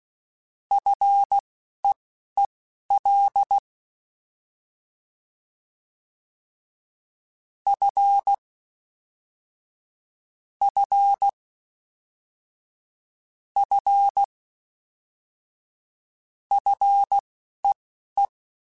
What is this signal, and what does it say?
Morse code, heard as a keyed tone, about 16 wpm, spaced as for 11 wpm, 800 Hz. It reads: FEEL F F F FEE
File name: Fs / Single F mp3s / Single F Prefixes 3x2 Twice @10